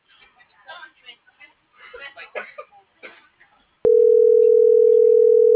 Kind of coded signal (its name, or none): telephone call progress tone